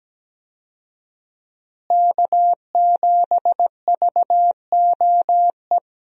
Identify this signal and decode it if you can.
Morse code, heard as a keyed tone, about 17 wpm, 700 Hz. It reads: K7VOE
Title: USA Call Signs - 5 Characters - Repeated 2x - 17wpm